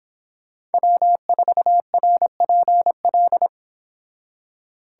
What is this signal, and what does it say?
Morse code, heard as a keyed tone, about 29 wpm, 700 Hz. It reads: W4RPL